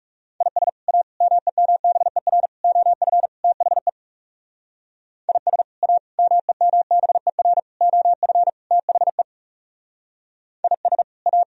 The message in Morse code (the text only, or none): IS A MEMBER OF THE IS A MEMBER OF THE IS A